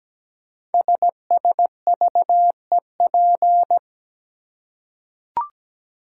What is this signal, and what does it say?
Morse code, heard as a keyed tone, about 17 wpm, 700 Hz. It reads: SSVEP
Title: Code Groups - 5 Letters - 17wpm